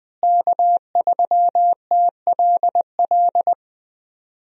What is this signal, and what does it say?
Morse code, heard as a keyed tone, about 20 wpm, 700 Hz. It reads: K3TLL
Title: USA Call Signs - 5 Characters - Repeated 3x - 20wpm